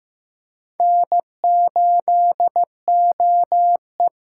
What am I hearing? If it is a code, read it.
Morse code, heard as a keyed tone, about 15 wpm, 700 Hz. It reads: N8OE
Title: USA Call Signs - 8x Character Spacing - 15wpm